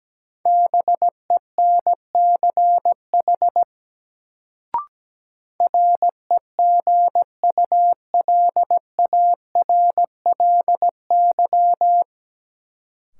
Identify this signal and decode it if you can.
Morse code, heard as a keyed tone, about 17 wpm, 700 Hz. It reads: BENCH REGULARLY
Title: Sentences from Top 1400 Words - New Word Drill - 17wpm